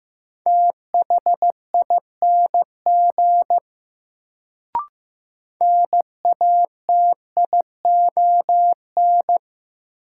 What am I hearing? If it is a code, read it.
Morse code, heard as a keyed tone, about 15 wpm, 700 Hz. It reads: THING NATION
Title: Sets of 3 Words - New Word Drill - 15wpm